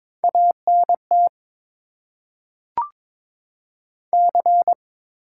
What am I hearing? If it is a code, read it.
Morse code, heard as a keyed tone, about 22 wpm, 700 Hz. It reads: ANT C
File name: Top 20 QSO Elements - 8x Character Spacing - 22wpm